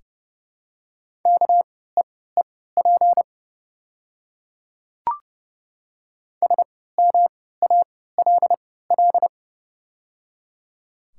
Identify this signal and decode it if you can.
Morse code, heard as a keyed tone, about 30 wpm, spaced as for 17 wpm, 700 Hz. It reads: KEEP SMALL
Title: 3x Character Spacing - Top 300 Words - 30wpm